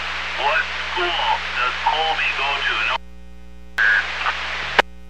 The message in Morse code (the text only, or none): none